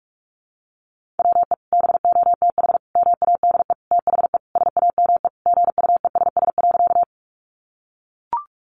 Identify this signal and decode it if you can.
Morse code, heard as a keyed tone, about 44 wpm, 700 Hz. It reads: WE BOTH MADE THE SAME GUESS.
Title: Sentences from Top 1500 Words 45wpm